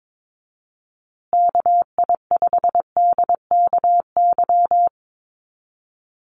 Morse code, heard as a keyed tone, about 22 wpm, 700 Hz. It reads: KI5DKY
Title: USA Call Signs Repeated 2x 22wpm